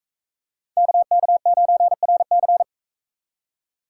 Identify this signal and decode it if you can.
Morse code, heard as a keyed tone, about 44 wpm, 700 Hz. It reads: KK9RC